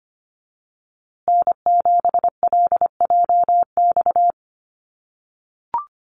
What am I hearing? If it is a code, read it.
Morse code, heard as a keyed tone, about 25 wpm, 700 Hz. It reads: N7LJX
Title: USA Call Signs Repeated 2x 25wpm